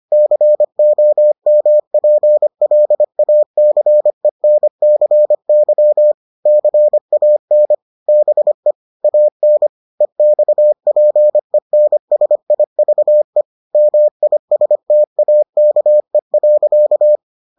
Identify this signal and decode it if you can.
Morse code, heard as a keyed tone, about 25 wpm, 600 Hz. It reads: COMPLACENCY CAN BE AN EXPENSIVE MISTAKE.